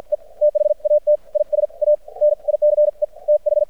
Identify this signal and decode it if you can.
Morse code, heard as a keyed tone, about 50 wpm, spaced as for 43 wpm, 600 Hz. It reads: E THAT IS A NICE TH
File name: archived / CRSnet